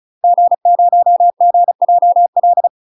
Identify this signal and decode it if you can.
Morse code, heard as a keyed tone, about 34 wpm, 700 Hz. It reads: G0GJL
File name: UK MASTER QSOs TRAINING/35WPM